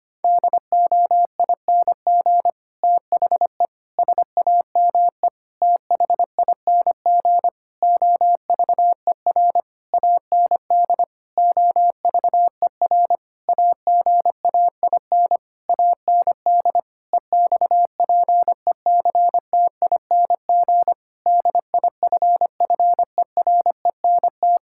Morse code, heard as a keyed tone, about 25 wpm, 700 Hz. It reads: DOING THE SAME THING OVER AND OVER AGAIN AND EXPECTING DIFFERENT
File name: practice / 25wpm / Sentences with 9-Word Phrases - 25wpm